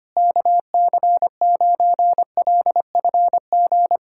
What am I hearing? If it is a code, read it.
Morse code, heard as a keyed tone, about 25 wpm, 700 Hz. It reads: KC9LFG
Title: USA Callsign DE Callsign - 6 Characters - 25wpm